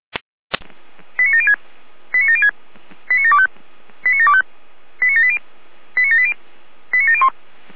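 Five-tone selective call (ZVEI), heard as quick runs of five tones, tones of about 80 ms. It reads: 89706 89706 89725 89725 89780 89780 89701
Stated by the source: ZVEI five-tone call received by SDR.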